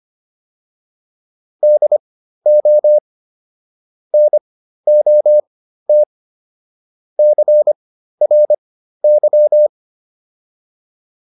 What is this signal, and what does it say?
Morse code, heard as a keyed tone, about 25 wpm, spaced as for 13 wpm, 600 Hz. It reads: DO NOT CRY